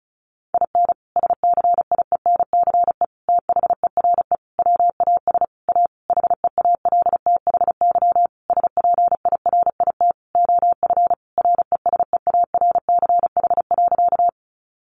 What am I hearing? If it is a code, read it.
Morse code, heard as a keyed tone, about 34 wpm, 700 Hz. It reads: IN SCIENCE THERE WAS A HEALTHY SPIRIT OF RESEARCH.
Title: Sentences from Top 1400 Words 35wpm